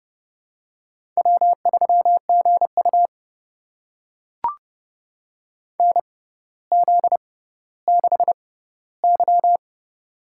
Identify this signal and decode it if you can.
Morse code, heard as a keyed tone, about 28 wpm, spaced as for 15 wpm, 700 Hz. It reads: W3GU NZ6Y
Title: USA Call Signs - 4 Characters - 6x Character Spacing - 30wpm